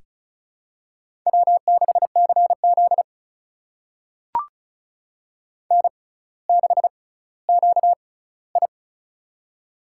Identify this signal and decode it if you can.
Morse code, heard as a keyed tone, about 36 wpm, spaced as for 18 wpm, 700 Hz. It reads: W6CZ N6QI
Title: USA Call Signs - 4 Characters - 6x Character Spacing - 35wpm